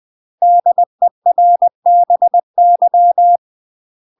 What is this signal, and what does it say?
Morse code, heard as a keyed tone, about 20 wpm, 700 Hz. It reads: DERBY
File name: UK MASTER QSOs TRAINING / 20WPM / UK MASTER QSOs 0072